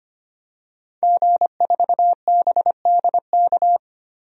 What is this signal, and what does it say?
Morse code, heard as a keyed tone, about 25 wpm, 700 Hz. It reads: G4BDK
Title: DXCC-100 Call Signs - 5 Characters - 25wpm